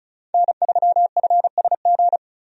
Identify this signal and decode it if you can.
Morse code, heard as a keyed tone, about 36 wpm, 700 Hz. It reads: N3FSG
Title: USA Call Signs - 5 Characters - 35wpm